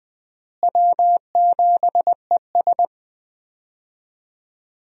Morse code, heard as a keyed tone, about 20 wpm, 700 Hz. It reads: W7ES
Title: USA Call Signs - 4 Characters - Repeated 2x - 20wpm